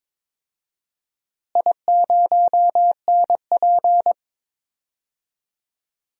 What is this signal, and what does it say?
Morse code, heard as a keyed tone, about 22 wpm, 700 Hz. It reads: I0NP
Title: DXCC-100 Call Signs - 4 Characters - Repeated 3x - 22wpm